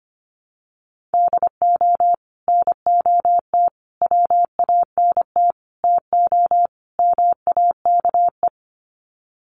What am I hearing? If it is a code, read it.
Morse code, heard as a keyed tone, about 25 wpm, 690 Hz. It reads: DO NOT WANT TO MAKE
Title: Sets of 5 Words Encore 25wpm